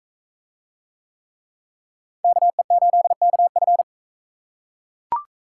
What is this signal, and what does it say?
Morse code, heard as a keyed tone, about 38 wpm, 700 Hz. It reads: KE8KF